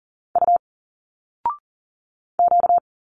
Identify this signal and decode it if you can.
Morse code, heard as a keyed tone, about 40 wpm, 700 Hz. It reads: U Q
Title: Single Letters 40wpm